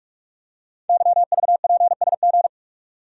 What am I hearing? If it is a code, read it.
Morse code, heard as a keyed tone, about 44 wpm, 700 Hz. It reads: YVPSG
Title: Code Groups - 5 Letters - 45wpm